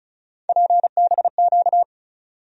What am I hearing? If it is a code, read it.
Morse code, heard as a keyed tone, about 34 wpm, 700 Hz. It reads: PBQ
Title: USA Call Sign Postfixes - 3 Letters 35wpm